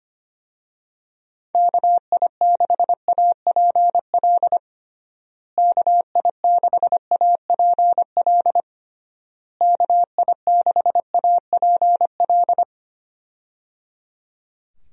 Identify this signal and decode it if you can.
Morse code, heard as a keyed tone, about 25 wpm, 700 Hz. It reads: KI6APL KI6APL KI6APL
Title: USA Call Signs - 6 Characters - Repeated 3x - 25wpm